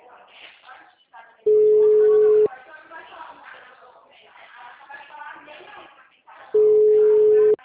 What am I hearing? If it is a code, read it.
Ringback tone (Europe, 425 Hz), with speech.
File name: out- -2010- - - .16332